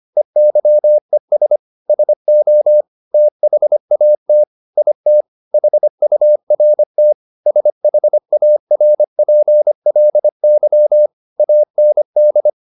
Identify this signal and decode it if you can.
Morse code, heard as a keyed tone, about 25 wpm, 600 Hz. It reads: EYES SO THAT IT HURT SHARPLY AND
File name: the old man and the sea part 1 25wpm